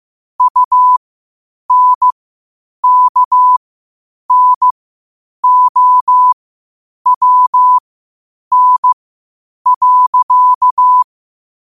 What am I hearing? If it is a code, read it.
Morse code, heard as a keyed tone, about 15 wpm, spaced as for 8 wpm, 1000 Hz. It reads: UNKNOWN.
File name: qotdfiles / 8wpm